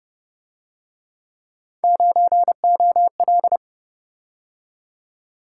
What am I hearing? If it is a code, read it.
Morse code, heard as a keyed tone, about 29 wpm, 700 Hz. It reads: 9OL
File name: Speed-Racing - Three Letter-Number Combinations - 25wpm